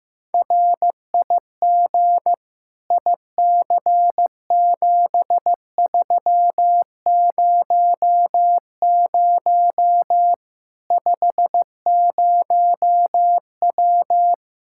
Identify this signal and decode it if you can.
Morse code, heard as a keyed tone, about 15 wpm, 700 Hz. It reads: RIG IC7300 50W
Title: Practice QSO - Exchange 2 - Info - 15wpm